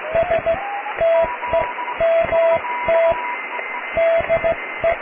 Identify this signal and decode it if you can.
Morse code, heard as a keyed tone, about 15 wpm, spaced as for 13 wpm, 660 Hz, with radio noise. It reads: STEMT DE
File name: cw / tekst(a)+QRM